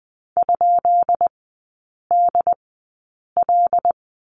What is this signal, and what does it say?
Morse code, heard as a keyed tone, about 20 wpm, spaced as for 8 wpm, 700 Hz. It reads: ?DL
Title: 133 Lesson 22e - Question Mark - Three Characters 20wpm